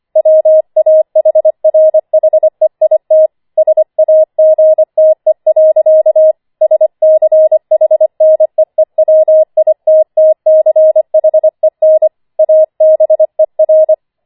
Morse code, heard as a keyed tone, about 25 wpm, 630 Hz. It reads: WAHRHEIT SAGTE. SCHNEEWITTCHEN ABER
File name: 25wpm